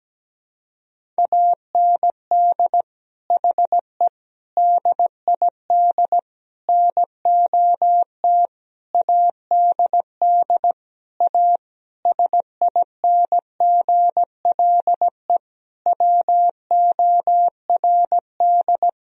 Morse code, heard as a keyed tone, about 17 wpm, 700 Hz. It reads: AND HE DID NOT ADD A SINGLE WORD
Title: Sentences from Top 1000 Words - Review Part 1 - 17wpm